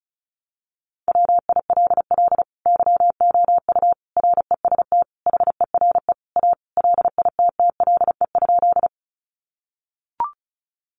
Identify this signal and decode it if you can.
Morse code, heard as a keyed tone, about 34 wpm, 700 Hz. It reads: WILL YOU REST HERE A LITTLE?
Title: Sentences from Top 500 Words 35wpm